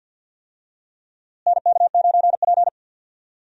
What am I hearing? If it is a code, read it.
Morse code, heard as a keyed tone, about 52 wpm, 700 Hz. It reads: NK9P
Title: USA Call Signs - 4 Characters - Repeated 2x - 50wpm